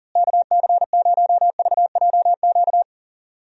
Morse code, heard as a keyed tone, about 39 wpm, 700 Hz. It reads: KC0VJQ
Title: USA Call Signs - 6 Characters - Repeated 3x - 40wpm